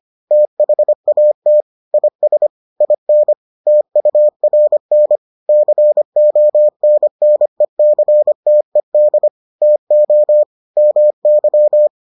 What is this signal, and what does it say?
Morse code, heard as a keyed tone, about 25 wpm, 600 Hz. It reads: THAT IS IN TURN CONNECTED TO MY